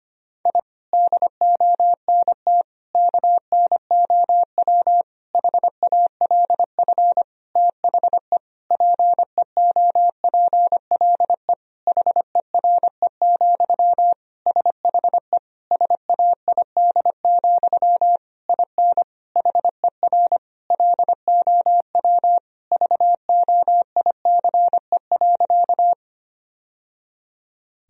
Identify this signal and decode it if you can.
Morse code, heard as a keyed tone, about 25 wpm, 700 Hz. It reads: I DONT KNOW HALF THE PEOPLE HERE, SHE SAID, IN HER LOW VOICE.
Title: Sentences from Top 1000 Words - Review Part 1 - 25wpm